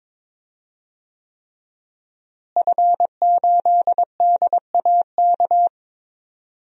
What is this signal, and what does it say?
Morse code, heard as a keyed tone, about 23 wpm, 700 Hz. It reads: F8DAK
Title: DXCC-100 Call Signs - 5 Characters - Repeated 3x - 22wpm